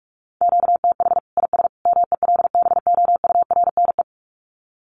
Morse code, heard as a keyed tone, about 44 wpm, 700 Hz. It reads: QTH IS MELBOURNE
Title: Common QSO Phrases 45wpm